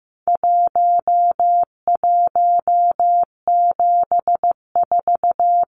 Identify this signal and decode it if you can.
Morse code, heard as a keyed tone, about 15 wpm, 700 Hz. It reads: 1174
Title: Numbers - Contest Style - 4-Digit 15wpm